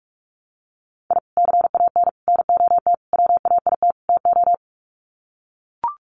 Morse code, heard as a keyed tone, about 44 wpm, 700 Hz. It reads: I CAN NOT WAIT TO
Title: Sets of 5 Words Encore 45wpm